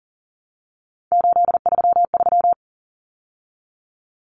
Morse code, heard as a keyed tone, about 40 wpm, 700 Hz. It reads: 833
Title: Numbers - 3-Digit 40wpm